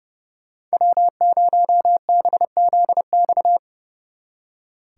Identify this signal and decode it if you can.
Morse code, heard as a keyed tone, about 36 wpm, 700 Hz. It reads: W0BZX
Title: Speed Racing - USA 1x3 Call Signs 30wpm